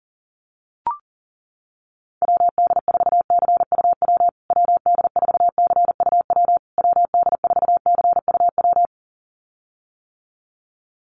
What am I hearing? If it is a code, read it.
Morse code, heard as a keyed tone, about 40 wpm, 700 Hz. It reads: WD4CUW WD4CUW WD4CUW
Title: USA Call Signs Repeated 3x 40wpm